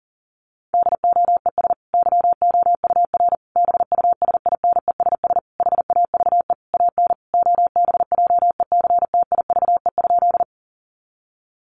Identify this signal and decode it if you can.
Morse code, heard as a keyed tone, about 40 wpm, 700 Hz. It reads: DOES YOUR BUSINESS HAVE AN OBJECTIVE?
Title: Sentences from Top 1200 Words 40wpm